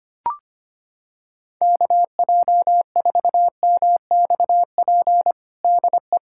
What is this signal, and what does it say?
Morse code, heard as a keyed tone, about 25 wpm, 700 Hz. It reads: KJ4MXP DE
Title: USA Callsign DE Callsign 25wpm